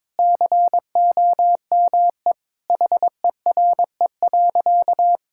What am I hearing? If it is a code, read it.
Morse code, heard as a keyed tone, about 21 wpm, 700 Hz. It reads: COME HERE.
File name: practice / 22wpm / Sentences with 2-Word Phrases - 22wpm